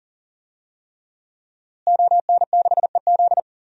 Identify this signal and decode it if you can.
Morse code, heard as a keyed tone, about 40 wpm, 700 Hz. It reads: ON6EZ
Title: DXCC-100 Call Signs - 5 Characters - Repeated 3x - 40wpm